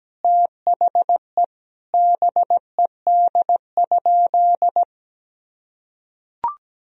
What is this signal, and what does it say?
Morse code, heard as a keyed tone, about 17 wpm, 700 Hz. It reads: THE BED?
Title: Sentences from Top 1000 Words - Review Part 4 - 17wpm